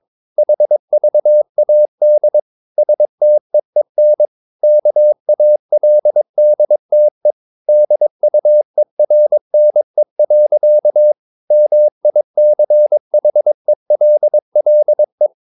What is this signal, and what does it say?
Morse code, heard as a keyed tone, about 22 wpm, 600 Hz. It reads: HVAD STEEN KALDTE DUERNE. MICHELLE
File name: Ferie QRS 22wpm